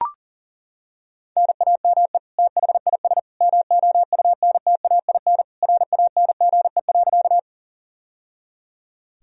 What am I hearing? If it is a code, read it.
Morse code, heard as a keyed tone, about 37 wpm, 700 Hz. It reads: NAME THIS MOUNTAIN RANGE.
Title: Sentences from Top 1000 Words - Review Part 4 - 40wpm